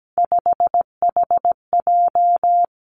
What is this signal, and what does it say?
Morse code, heard as a keyed tone, about 17 wpm, 700 Hz. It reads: 5HJ
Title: Three Challenging Letter-Number Combinations 17wpm